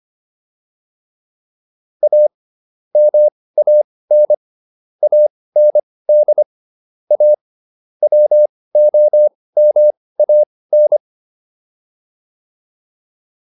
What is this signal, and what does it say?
Morse code, heard as a keyed tone, about 25 wpm, spaced as for 18 wpm, 600 Hz. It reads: A MAN AND A WOMAN